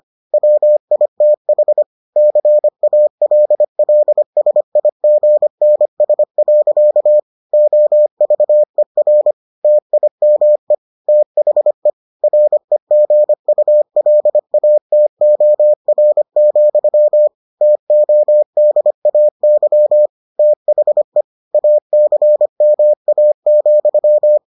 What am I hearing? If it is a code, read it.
Morse code, heard as a keyed tone, about 25 wpm, 600 Hz. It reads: WITH CALLSIGNS. OVER TIME THE REGULATOR, TODAY THE ACMA,